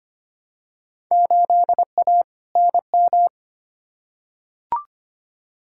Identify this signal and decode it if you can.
Morse code, heard as a keyed tone, about 25 wpm, 700 Hz. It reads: 8A NM
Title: Field Day Exchanges Encore - 25wpm